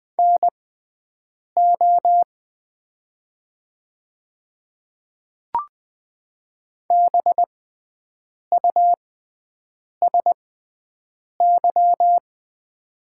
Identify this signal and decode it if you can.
Morse code, heard as a keyed tone, about 20 wpm, spaced as for 7 wpm, 690 Hz. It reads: NO BUSY
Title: 6x Character Spacing - Top 500 Words - 20wpm